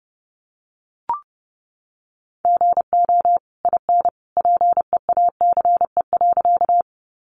Morse code, heard as a keyed tone, about 30 wpm, 700 Hz. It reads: GO IN PEACE.
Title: Sentences from Top 1500 Words 30wpm